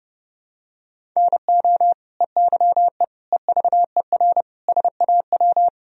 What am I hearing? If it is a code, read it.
Morse code, heard as a keyed tone, about 32 wpm, 700 Hz. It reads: NO EYE EVER SAW
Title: Sentences from Top 1000 Words - Review Part 1 - 30wpm